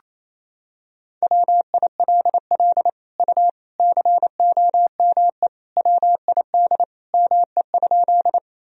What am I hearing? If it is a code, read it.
Morse code, heard as a keyed tone, about 28 wpm, 700 Hz. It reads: WILL U COME WID ME?
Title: Sentences from Top 200 Words and Abbreviations 28wpm